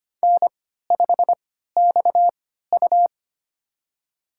Morse code, heard as a keyed tone, about 25 wpm, spaced as for 14 wpm, 700 Hz. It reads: N5XU
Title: USA Call Signs - 4 Characters - 3x Character Spacing - 25wpm